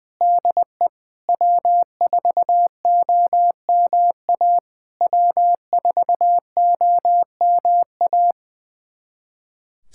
Morse code, heard as a keyed tone, about 21 wpm, 700 Hz. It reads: DE W4OMA W4OMA
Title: SOTA QSO Practice 20wpm